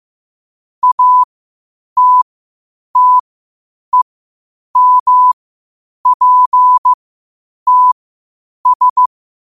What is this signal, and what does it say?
Morse code, heard as a keyed tone, about 15 wpm, spaced as for 8 wpm, 1000 Hz. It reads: ATTEMPTS
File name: 8wpm